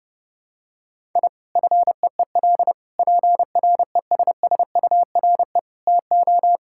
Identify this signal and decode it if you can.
Morse code, heard as a keyed tone, about 30 wpm, 700 Hz. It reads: I FEEL PRESSURE TO
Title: Sentences from Top 1000 Words - Review Part 6 - 30wpm